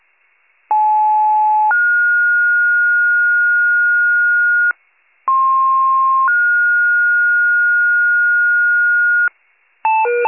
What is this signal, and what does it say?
Two-tone page, heard as a one-second tone, then a three-second tone, about 860 then 1430 Hz, about 1060 then 1430 Hz. It is heard from 0.7 to 9.3 s.